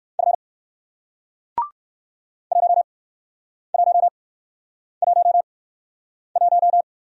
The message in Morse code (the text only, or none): F 1111